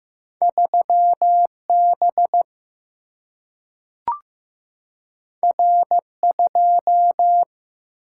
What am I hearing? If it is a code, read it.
Morse code, heard as a keyed tone, about 15 wpm, 700 Hz. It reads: 3B R2
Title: Single-Letter Single-Number 15wpm